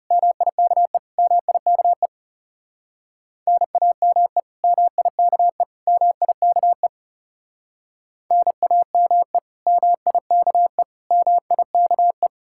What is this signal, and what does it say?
Morse code, heard as a keyed tone, about 37 wpm, 700 Hz. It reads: MIKE MIKE NAME MIKE MIKE NAME MIKE MIKE
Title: Speed-Racing Common QSO Phrases - 30wpm